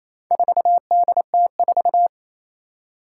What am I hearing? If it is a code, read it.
Morse code, heard as a keyed tone, about 28 wpm, 700 Hz. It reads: 4DT4
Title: Four Challenging Letter-Number Combinations 28wpm